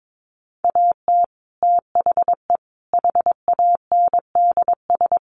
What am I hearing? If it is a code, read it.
Morse code, heard as a keyed tone, about 22 wpm, 700 Hz. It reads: AT THE HANDS
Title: Sets of 5 Words Encore2 22wpm